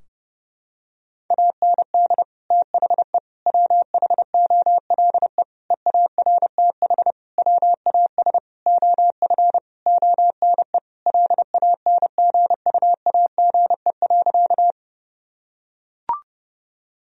Morse code, heard as a keyed tone, about 32 wpm, 700 Hz. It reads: AND THE WHOLE EARTH WAS OF ONE LANGUAGE.
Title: Sentences from Top 1000 Words - Review Part 3 - 30wpm